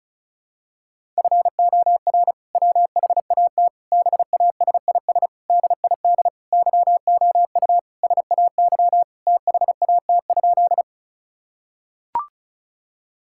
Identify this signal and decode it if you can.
Morse code, heard as a keyed tone, about 34 wpm, 700 Hz. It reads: FOR WHAT BASIS DID YOU SAY THAT?
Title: Sentences from Top 1000 Words - Review Part 5 - 35wpm